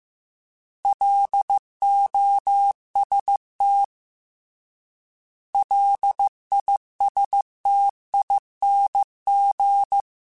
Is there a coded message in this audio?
Yes: Morse code from 0.8 to 10.0 s, heard as a keyed tone, about 15 wpm, 800 Hz. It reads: LOST LISTING